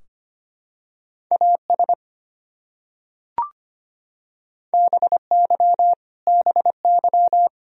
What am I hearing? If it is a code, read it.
Morse code, heard as a keyed tone, about 25 wpm, 700 Hz. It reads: AS BY BY
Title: Top 100 Words Repeated 25wpm